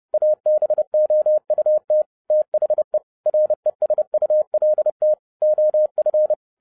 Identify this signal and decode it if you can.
Morse code, heard as a keyed tone, about 30 wpm, 610 Hz. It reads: ABOUT THE RESULT OF